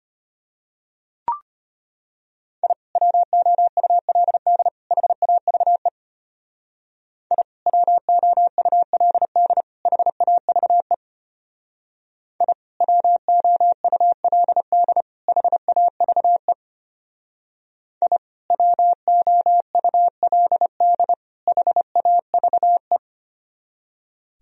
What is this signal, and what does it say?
Morse code, heard as a keyed tone, about 33 wpm, 700 Hz. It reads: I WOULD HAVE I WOULD HAVE I WOULD HAVE I WOULD HAVE